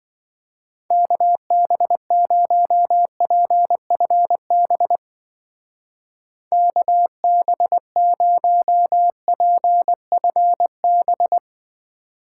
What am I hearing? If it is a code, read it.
Morse code, heard as a keyed tone, about 23 wpm, 700 Hz. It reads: KB0PFB KB0PFB
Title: Speed Racing - USA Call Signs 20wpm